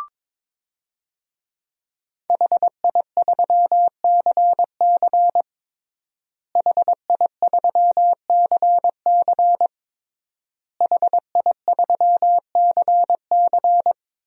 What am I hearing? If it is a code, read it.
Morse code, heard as a keyed tone, about 22 wpm, 700 Hz. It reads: HI3CC HI3CC HI3CC
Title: DXCC-100 Call Signs - 5 Characters - Repeated 3x - 22wpm